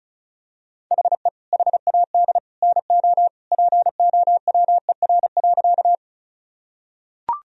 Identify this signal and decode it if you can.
Morse code, heard as a keyed tone, about 34 wpm, 700 Hz. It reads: HE HAD NO POWER.
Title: Sentences from Top 1000 Words - Review Part 2 - 35wpm